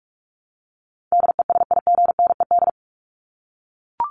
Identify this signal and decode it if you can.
Morse code, heard as a keyed tone, about 44 wpm, 700 Hz. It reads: DESIGNED
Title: ED Suffix - Top 100 Words with Base Words 45wpm